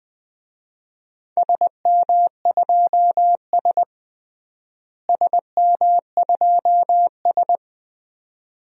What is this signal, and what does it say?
Morse code, heard as a keyed tone, about 20 wpm, 700 Hz. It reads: SM2S SM2S
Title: DXCC-100 Call Signs - 4 Characters - Repeated 2x - 20wpm